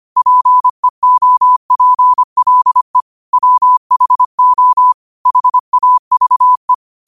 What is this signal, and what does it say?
Morse code, heard as a keyed tone, about 25 wpm, 1000 Hz. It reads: PEOPLE WHO HAVE
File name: 25wpm